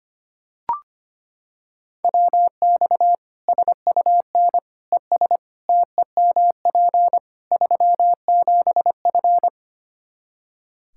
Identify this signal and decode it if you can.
Morse code, heard as a keyed tone, about 25 wpm, 700 Hz. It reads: WX SUN ES TEMP 37F
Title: Practice QSO - Exchange 2 - Info - 25wpm